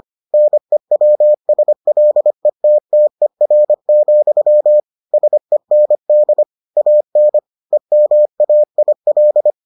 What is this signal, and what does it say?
Morse code, heard as a keyed tone, about 25 wpm, 600 Hz. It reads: NEWSLETTER, SEND AN EMAIL